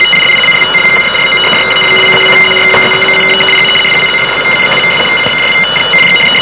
Two-tone teleprinter signal, about 850 Hz shift.